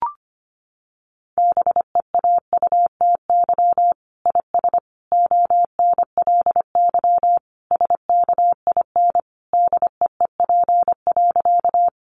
Morse code, heard as a keyed tone, about 25 wpm, 700 Hz. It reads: BEAUTY IS ONLY SKIN DEEP.
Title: Sentences from Top 1300 Words 25wpm